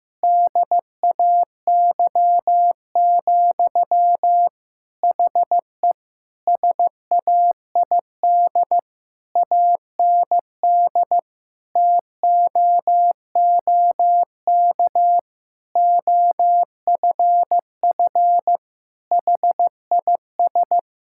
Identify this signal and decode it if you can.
Morse code, heard as a keyed tone, about 15 wpm, 700 Hz. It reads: DAY, HE SAID AND TOOK OFF HIS
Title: Sentences from Top 1600 Words 15wpm